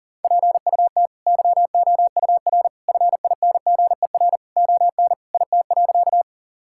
Morse code, heard as a keyed tone, about 40 wpm, 700 Hz. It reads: PUT YOUR FINGER ON IT.
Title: Sentences from Top 1600 Words 40wpm